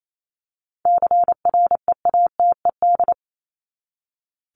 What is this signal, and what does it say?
Morse code, heard as a keyed tone, about 28 wpm, 700 Hz. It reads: CREATED
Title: ED Suffix - Top 100 Words with Base Words 28wpm